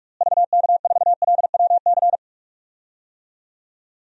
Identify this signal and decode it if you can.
Morse code, heard as a keyed tone, about 46 wpm, 700 Hz. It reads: VK4LWC
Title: DXCC-100 Call Signs - 6 Characters - Repeated 3x - 45wpm